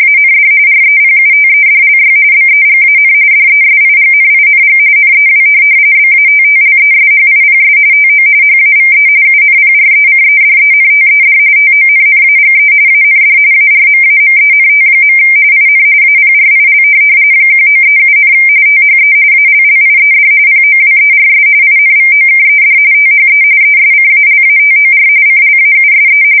RTTY, about 45.45 baud. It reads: AND REPUTEDLY BEGAN DESIGNING NEW BATTLESHIPS TO HAVE UNPRECEDENTED TONNAGE AND ARMAMENT. IN APRIL 1937, THE ALLIED POWERS INVOKED THE ESCALATOR CLAUSE
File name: fsk170 / USS-New-Jersey